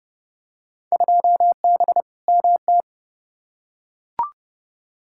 Morse code, heard as a keyed tone, about 30 wpm, 700 Hz. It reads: 2B MT
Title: Field Day Exchanges Encore - 30wpm